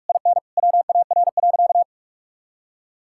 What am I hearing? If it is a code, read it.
Morse code, heard as a keyed tone, about 46 wpm, 700 Hz. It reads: IN WAR.
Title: Sentences from Top 1000 Words - Review Part 6 - 45wpm